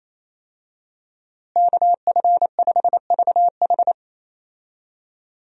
Morse code, heard as a keyed tone, about 28 wpm, 700 Hz. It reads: KF5VH
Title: USA Call Signs - 5 Characters - Repeated 3x - 28wpm